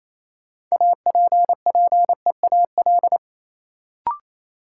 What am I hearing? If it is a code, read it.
Morse code, heard as a keyed tone, about 27 wpm, 700 Hz. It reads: APPEAL
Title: Sentences from Top 1300 Words - New Word Drill - 28wpm